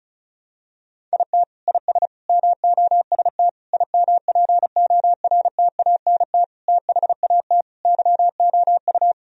Morse code, heard as a keyed tone, about 36 wpm, 700 Hz. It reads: IT IS MOST IMPORTANT THAT YOU
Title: Sentences from Top 1000 Words - Review Part 1 - 35wpm